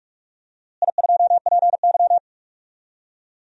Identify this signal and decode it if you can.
Morse code, heard as a keyed tone, about 44 wpm, 700 Hz. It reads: I2PY
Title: DXCC-100 Call Signs - 4 Characters - Repeated 3x - 45wpm